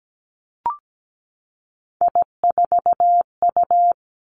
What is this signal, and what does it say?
Morse code, heard as a keyed tone, about 17 wpm, 700 Hz. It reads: I4U
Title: Three Challenging Letter-Number Combinations 17wpm